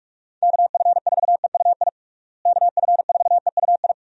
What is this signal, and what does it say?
Morse code, heard as a keyed tone, about 42 wpm, 700 Hz. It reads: KF4EUI KF4EUI
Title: USA Call Signs - 6 Characters - Repeated 2x - 45wpm